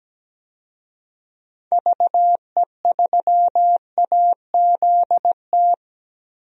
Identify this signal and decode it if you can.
Morse code, heard as a keyed tone, about 17 wpm, 700 Hz. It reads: VE3AZT
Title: DXCC-100 Call Signs - 6 Characters - 17wpm